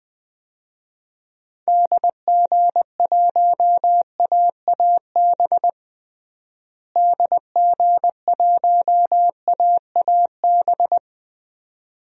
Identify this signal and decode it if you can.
Morse code, heard as a keyed tone, about 20 wpm, 700 Hz. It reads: DG1AAB DG1AAB
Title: DXCC-100 Call Signs - 6 Characters - Repeated 2x - 20wpm